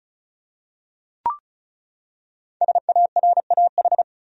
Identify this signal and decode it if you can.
Morse code, heard as a keyed tone, about 36 wpm, 700 Hz. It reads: SARAH
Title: USA Common Names - Max 5 Letters - 35wpm